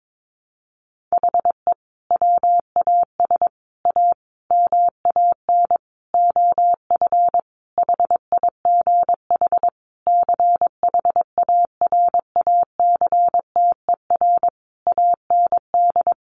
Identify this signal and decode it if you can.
Morse code, heard as a keyed tone, about 22 wpm, 700 Hz. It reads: HE WAS A MAN OF HIGH CHARACTER AND
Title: Sentences from Top 1200 Words 22wpm